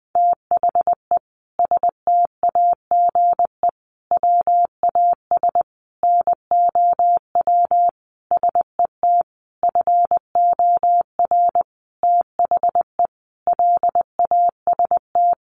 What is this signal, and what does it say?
Morse code, heard as a keyed tone, about 20 wpm, 700 Hz. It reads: THE STAGE WAS NOW SET FOR THE LAST
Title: Sentences from Top 1500 Words 20wpm